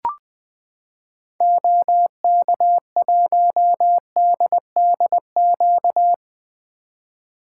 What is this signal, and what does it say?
Morse code, heard as a keyed tone, about 20 wpm, 700 Hz. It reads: OK1DDQ
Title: 2023 SOTA Activator Call Signs - 20wpm